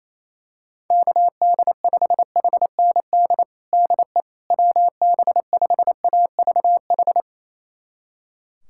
Morse code, heard as a keyed tone, about 28 wpm, 700 Hz. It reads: KD5HND DE WB5AVH
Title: USA Callsign DE Callsign - 6 Characters - 28wpm